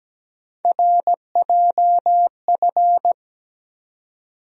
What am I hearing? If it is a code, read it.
Morse code, heard as a keyed tone, about 17 wpm, 700 Hz. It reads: RJF
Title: USA Call Sign Postfixes - 3 Letters 17wpm